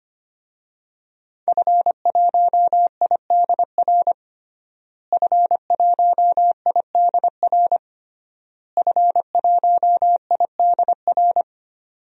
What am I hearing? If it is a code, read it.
Morse code, heard as a keyed tone, about 25 wpm, 700 Hz. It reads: F1IDR F1IDR F1IDR
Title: DXCC-100 Call Signs - 5 Characters - Repeated 3x - 25wpm